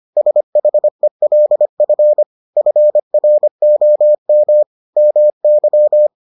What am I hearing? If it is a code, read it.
Morse code, heard as a keyed tone, about 25 wpm, 600 Hz. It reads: SHELF FROM MY